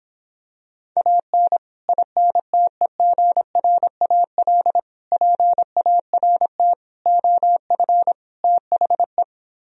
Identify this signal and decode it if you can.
Morse code, heard as a keyed tone, about 26 wpm, 700 Hz. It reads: AN INTEGRAL PART OF THE